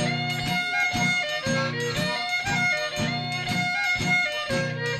Emergency Alert System (SAME) tones, heard as harsh buzzing bursts. It is heard from 0.6 to 4.3 s.